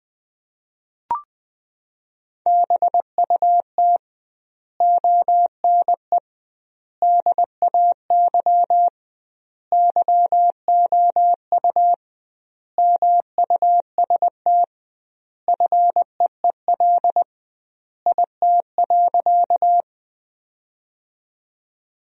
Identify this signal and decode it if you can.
Morse code, heard as a keyed tone, about 20 wpm, 700 Hz. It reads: BUT ONE DAY YOU MUST FEEL IT.
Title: 2x Word-Spacing - Sentences from Top 400 Words - 20wpm